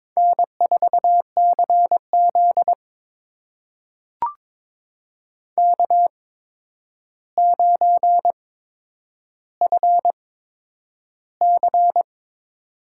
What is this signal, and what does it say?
Morse code, heard as a keyed tone, about 22 wpm, spaced as for 10 wpm, 700 Hz. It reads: N4CZ K9FC
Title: USA Call Signs - 4 Characters - 8x Character Spacing - 22wpm